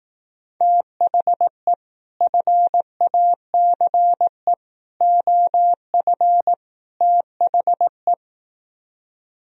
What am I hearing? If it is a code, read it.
Morse code, heard as a keyed tone, about 18 wpm, 700 Hz. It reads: THE FACE OF THE
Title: Speed-Racing - Sets of 4 Words 15wpm